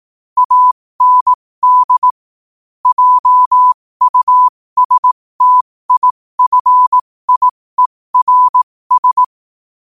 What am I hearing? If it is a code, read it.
Morse code, heard as a keyed tone, about 18 wpm, spaced as for 15 wpm, 1000 Hz. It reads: AND JUSTIFIERS
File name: qotdfiles / 15wpm